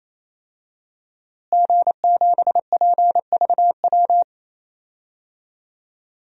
Morse code, heard as a keyed tone, about 28 wpm, 700 Hz. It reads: G7PVW
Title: DXCC-100 Call Signs - 5 Characters - Repeated 3x - 28wpm